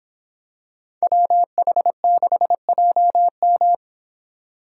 Morse code, heard as a keyed tone, about 26 wpm, 700 Hz. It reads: WH6JM